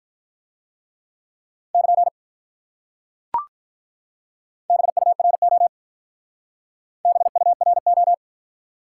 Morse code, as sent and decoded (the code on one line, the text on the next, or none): -.--. / -... ..- .-. --- / -... ..- .-. ---
<KN> BURO BURO